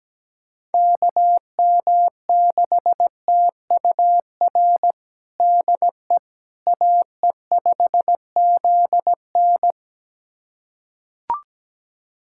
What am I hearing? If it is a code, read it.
Morse code, heard as a keyed tone, about 17 wpm, 700 Hz. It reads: KM6TUR DE AE5ZN
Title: USA Callsign DE Callsign 17wpm